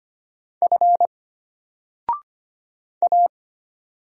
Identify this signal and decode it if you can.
Morse code, heard as a keyed tone, about 25 wpm, 700 Hz. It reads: F A
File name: Single-Letter-Number-25wpm